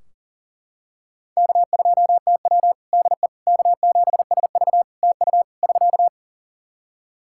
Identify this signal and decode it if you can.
Morse code, heard as a keyed tone, about 40 wpm, 700 Hz. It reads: K2TW DE K7SV TU <SK>
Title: K1USN SST QSO Practice 40wpm